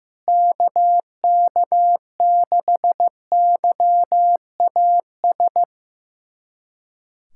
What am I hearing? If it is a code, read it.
Morse code, heard as a keyed tone, about 15 wpm, 700 Hz. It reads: KK6YAS
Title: USA Callsign DE Callsign - 6 Characters - 15wpm